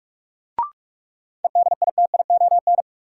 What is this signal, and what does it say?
Morse code, heard as a keyed tone, about 46 wpm, 700 Hz. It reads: EDITION
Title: Sets of 5 Words - New Word Drill - 45wpm